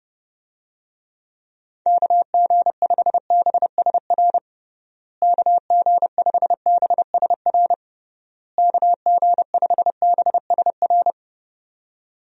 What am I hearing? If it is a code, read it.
Morse code, heard as a keyed tone, about 32 wpm, 700 Hz. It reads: KG5BSR KG5BSR KG5BSR
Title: USA Call Signs - 6 Characters - Repeated 3x - 30wpm